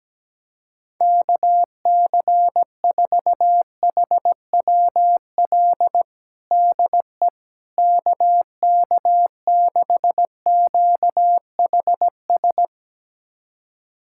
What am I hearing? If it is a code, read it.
Morse code, heard as a keyed tone, about 17 wpm, 700 Hz. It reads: KC4HWL DE KK6QHS
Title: USA Callsign DE Callsign 17wpm